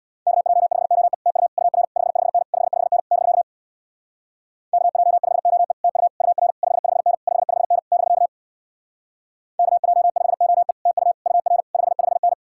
Morse code, heard as a keyed tone, about 64 wpm, 700 Hz. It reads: KJ5OE TU UR 55N 55N <BK> KJ5OE TU UR 55N 55N <BK> KJ5OE TU UR 55N